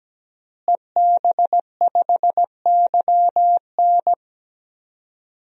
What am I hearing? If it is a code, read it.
Morse code, heard as a keyed tone, about 17 wpm, 700 Hz. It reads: EB5YN
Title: DXCC-100 Call Signs - 5 Characters - Repeated 2x - 17wpm